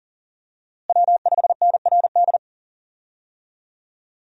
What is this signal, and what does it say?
Morse code, heard as a keyed tone, about 37 wpm, 700 Hz. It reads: W5NRD
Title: USA Call Signs - 5 Characters - Repeated 2x - 40wpm